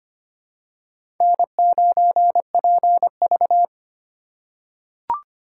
Morse code, heard as a keyed tone, about 25 wpm, 700 Hz. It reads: N9PV
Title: USA Call Signs - 4 Characters - 2x Character Spacing - 25wpm